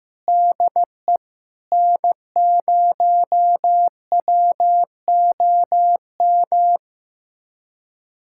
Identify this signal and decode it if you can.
Morse code, heard as a keyed tone, about 15 wpm, 700 Hz. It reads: DE N0WOM
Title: USA Callsign DE Callsign - 5 Characters - 15wpm